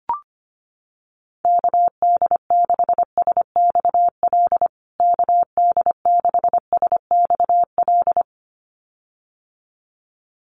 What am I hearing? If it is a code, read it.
Morse code, heard as a keyed tone, about 25 wpm, 700 Hz. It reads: KD6SXL KD6SXL
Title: USA Call Signs Repeated 2x 25wpm